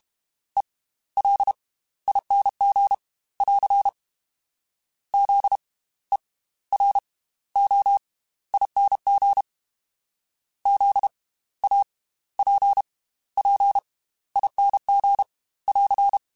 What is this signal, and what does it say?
Morse code, heard as a keyed tone, about 32 wpm, spaced as for 16 wpm, 800 Hz. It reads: ELING <AR> ZEROING ZAPPING <AR>